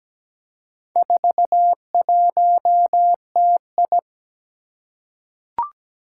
Morse code, heard as a keyed tone, about 17 wpm, 700 Hz. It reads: 41TI
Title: Four Challenging Letter-Number Combinations 17wpm